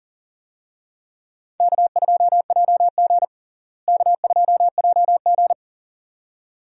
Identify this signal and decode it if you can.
Morse code, heard as a keyed tone, about 40 wpm, 700 Hz. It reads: K2JG K2JG
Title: USA Call Signs - 4 Characters - Repeated 2x - 40wpm